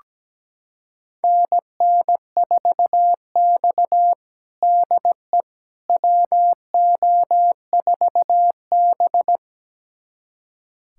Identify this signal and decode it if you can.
Morse code, heard as a keyed tone, about 17 wpm, 700 Hz. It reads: NN4X DE WO4B
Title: USA Callsign DE Callsign - 4 Characters - 17wpm